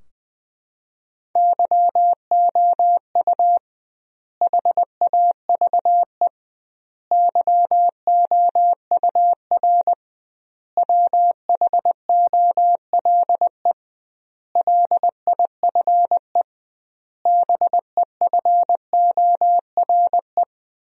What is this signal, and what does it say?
Morse code, heard as a keyed tone, about 20 wpm, 700 Hz. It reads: YOU HAVE YOUR WHOLE LIFE BEFORE
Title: 2x Word-Spacing - Sentences from Top 400 Words - 20wpm